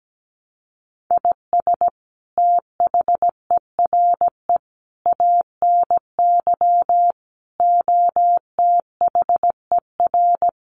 Morse code, heard as a keyed tone, about 17 wpm, 700 Hz. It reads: IS THERE ANY OTHER
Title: Sentences from Top 1100 Words 17wpm